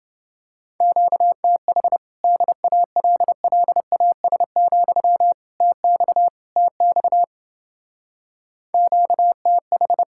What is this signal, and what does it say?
Morse code, heard as a keyed tone, about 28 wpm, 700 Hz. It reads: QTH DALLAS, TX TX QTH